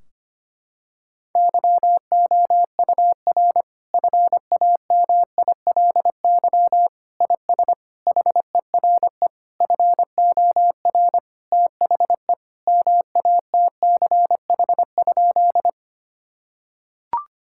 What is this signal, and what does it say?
Morse code, heard as a keyed tone, about 25 wpm, 700 Hz. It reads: YOUR FAMILY IS HERE FOR THE MATCH?
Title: Sentences from Top 1600 Words 25wpm